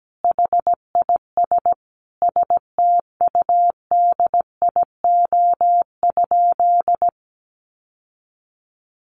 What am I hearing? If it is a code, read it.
Morse code, heard as a keyed tone, about 17 wpm, 700 Hz. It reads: HIS STUDIO?
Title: Sentences from Top 1200 Words 17wpm